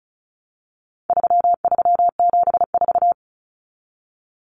Morse code, heard as a keyed tone, about 34 wpm, 700 Hz. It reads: 3374
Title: Numbers - 4-Digit 35wpm